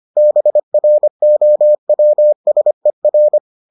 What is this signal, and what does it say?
Morse code, heard as a keyed tone, about 25 wpm, 600 Hz. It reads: BROWSER